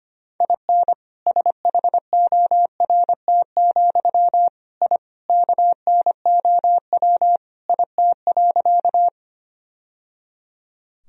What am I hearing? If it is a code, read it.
Morse code, heard as a keyed tone, about 25 wpm, 700 Hz. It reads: IN SHORT, I KNOW IT.
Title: Sentences from Top 1000 Words - Review Part 2 - 25wpm